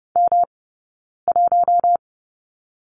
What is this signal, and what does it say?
Morse code, heard as a keyed tone, about 30 wpm, spaced as for 9 wpm, 690 Hz. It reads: M1
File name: Lesson 18d - Letter m - Two Characters 30wpm